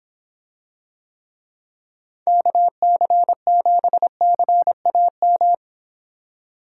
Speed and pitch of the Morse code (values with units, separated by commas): 26 wpm, 700 Hz